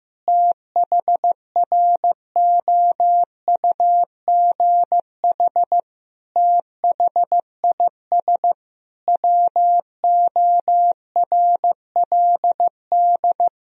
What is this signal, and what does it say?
Morse code, heard as a keyed tone, about 15 wpm, 700 Hz. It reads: THROUGH THIS WORLD
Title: Sentences from Top 1000 Words - Review Part 2 - 15wpm